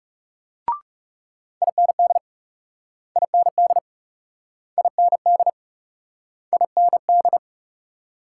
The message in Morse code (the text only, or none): IND IND IND IND